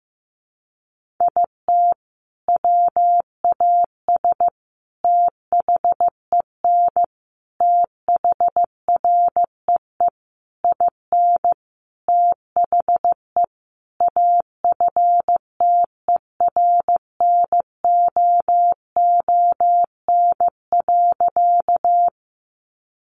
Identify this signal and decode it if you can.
Morse code, heard as a keyed tone, about 15 wpm, 700 Hz. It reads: IT WAS THEN THREE IN THE AFTERNOON.
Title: Sentences from Top 1500 Words 15wpm